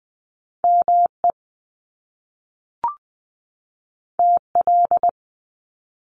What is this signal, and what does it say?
Morse code, heard as a keyed tone, about 20 wpm, 700 Hz. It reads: ME TL
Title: Top Two Letter Combinations in Words 20wpm